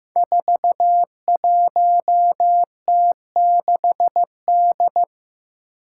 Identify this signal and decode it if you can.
Morse code, heard as a keyed tone, about 15 wpm, 700 Hz. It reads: 41T6D
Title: Five Challenging Letter-Number Combinations 15wpm